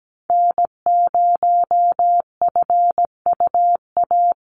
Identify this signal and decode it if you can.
Morse code, heard as a keyed tone, about 17 wpm, 700 Hz. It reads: N0FUA
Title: USA Call Signs 17wpm